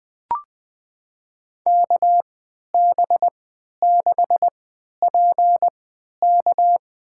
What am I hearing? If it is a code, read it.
Morse code, heard as a keyed tone, about 20 wpm, spaced as for 11 wpm, 700 Hz. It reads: KB6PK
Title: USA Call Signs - 3x Character Spacing - 20wpm